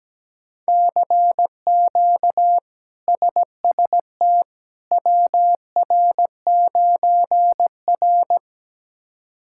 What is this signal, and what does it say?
Morse code, heard as a keyed tone, about 17 wpm, 700 Hz. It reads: CQ SST WR9R
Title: K1USN SST QSO Practice - 17wpm